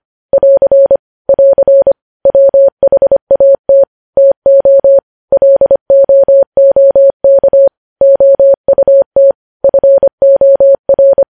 Morse code, heard as a keyed tone, about 25 wpm, 550 Hz. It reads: <AR> <AR> WHAT TO LOOK OUT FOR
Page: news-headlines-in-morse-code-at-25-wpm